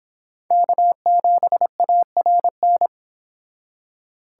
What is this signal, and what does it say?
Morse code, heard as a keyed tone, about 31 wpm, 700 Hz. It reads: K7ARN